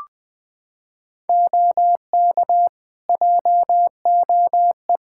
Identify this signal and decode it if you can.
Morse code, heard as a keyed tone, about 20 wpm, 700 Hz. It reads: OK JOE
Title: Practice QSO - Exchange 3 - More Info - 20wpm